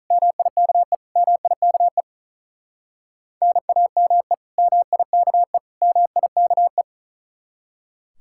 Morse code, heard as a keyed tone, about 45 wpm, 700 Hz. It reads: MIKE MIKE NAME MIKE MIKE